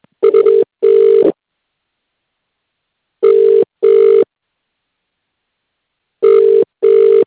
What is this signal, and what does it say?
Ringback tone (UK), 400 + 450 Hz.